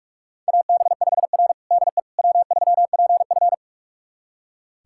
Morse code, heard as a keyed tone, about 42 wpm, 700 Hz. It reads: AB5R DE W3PF
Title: USA Callsign DE Callsign - 4 Characters - 45wpm